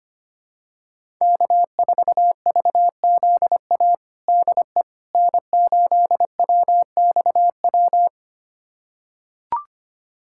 Morse code, heard as a keyed tone, about 25 wpm, 700 Hz. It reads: K4VZA DE N8WXW
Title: USA Callsign DE Callsign - 5 Characters - 25wpm